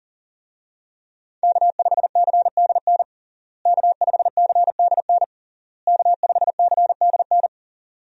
Morse code, heard as a keyed tone, about 40 wpm, 700 Hz. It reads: K5CDN K5CDN K5CDN
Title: USA Call Signs - 5 Characters - Repeated 3x - 40wpm